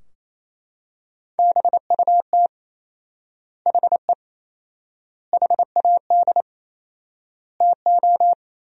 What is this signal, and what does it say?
Morse code, heard as a keyed tone, about 28 wpm, 700 Hz. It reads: BUT HE HAD TO
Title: 4x Word-Spacing - Sentences from Top 100 Words - 28wpm